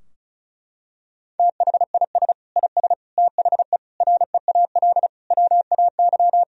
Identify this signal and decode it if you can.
Morse code, heard as a keyed tone, about 36 wpm, 700 Hz. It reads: THIS IS THE REAL WAY
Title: Sentences from Top 1000 Words - Review Part 2 - 35wpm